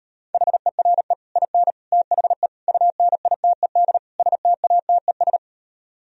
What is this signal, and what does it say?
Morse code, heard as a keyed tone, about 38 wpm, 700 Hz. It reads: HERE IN THE UNITED STATES